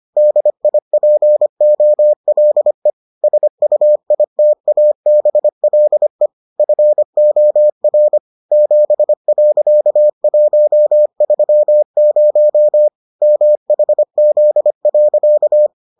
Morse code, heard as a keyed tone, about 25 wpm, 600 Hz. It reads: DIPOLE SUITABLE FOR 7.130 MHZ.